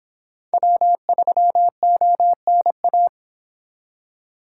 Morse code, heard as a keyed tone, about 31 wpm, 700 Hz. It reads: W3ONA